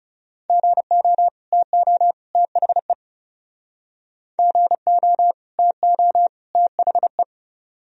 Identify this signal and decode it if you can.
Morse code, heard as a keyed tone, about 38 wpm, 700 Hz. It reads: GO TO THE GO TO THE